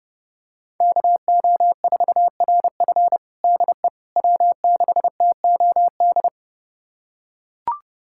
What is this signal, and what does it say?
Morse code, heard as a keyed tone, about 32 wpm, 700 Hz. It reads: KO4RF DE W6TOD
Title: USA Callsign DE Callsign - 5 Characters - 30wpm